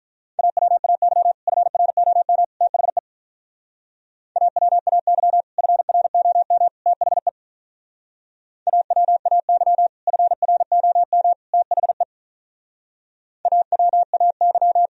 Morse code, heard as a keyed tone, about 46 wpm, 700 Hz. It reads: AWAY FROM THE AWAY FROM THE AWAY FROM THE AWAY